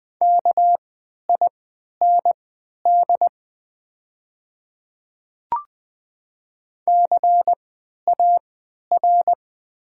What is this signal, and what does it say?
Morse code, heard as a keyed tone, about 21 wpm, spaced as for 11 wpm, 700 Hz. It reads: KIND CAR
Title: 3x Character Spacing - Top 300 Words - 20wpm